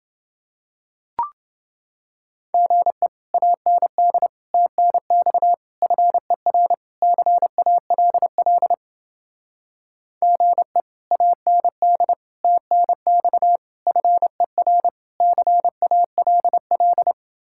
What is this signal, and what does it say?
Morse code, heard as a keyed tone, about 30 wpm, 700 Hz. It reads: GE AND TNX FER CALL GE AND TNX FER CALL